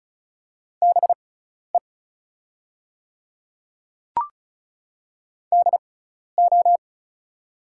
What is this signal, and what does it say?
Morse code, heard as a keyed tone, about 34 wpm, spaced as for 12 wpm, 700 Hz. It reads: BE DO
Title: 6x Character Spacing - Top 100 Words - 35wpm